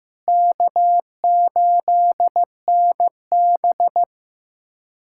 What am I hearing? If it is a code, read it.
Morse code, heard as a keyed tone, about 15 wpm, 700 Hz. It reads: K8NB
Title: USA Callsign DE Callsign - 4 Characters - 15wpm